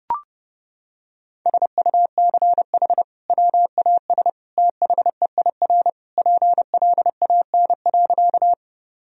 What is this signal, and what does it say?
Morse code, heard as a keyed tone, about 32 wpm, 700 Hz. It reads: SUCH WAS THEIR PLAN.
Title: Sentences from Top 1000 Words - Review Part 2 - 30wpm